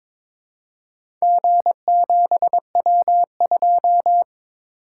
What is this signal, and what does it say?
Morse code, heard as a keyed tone, about 22 wpm, 700 Hz. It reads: G7W2
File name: practice / 22wpm / Four Letter-Number Combinations - 22wpm